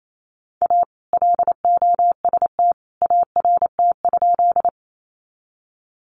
Morse code, heard as a keyed tone, about 28 wpm, 700 Hz. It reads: A LOST ART?
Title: Sentences from Top 1400 Words 28wpm